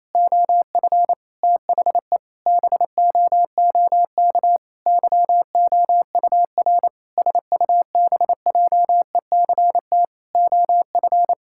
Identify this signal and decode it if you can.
Morse code, heard as a keyed tone, about 28 wpm, 700 Hz. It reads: OF THE BOOK YOUR SUBJECT OF
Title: Sentences from Top 1000 Words - Review Part 5 - 28wpm